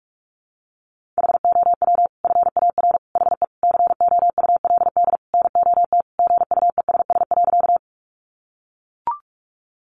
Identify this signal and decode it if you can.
Morse code, heard as a keyed tone, about 44 wpm, 700 Hz. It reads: HOW FAR HE COULD NOT GUESS.
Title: Sentences from Top 1500 Words 45wpm